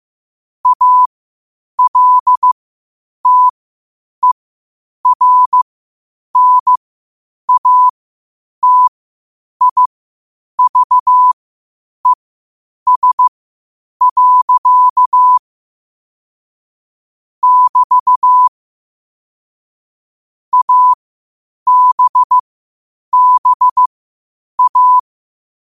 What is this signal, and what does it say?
Morse code, heard as a keyed tone, about 15 wpm, spaced as for 8 wpm, 1000 Hz. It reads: ALTERNATIVES. <BT> ABBA
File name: qotdfiles / 8wpm